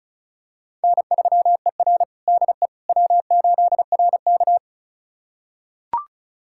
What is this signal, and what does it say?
Morse code, heard as a keyed tone, about 36 wpm, 700 Hz. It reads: N3ER DE W8RK
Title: USA Callsign DE Callsign - 4 Characters - 35wpm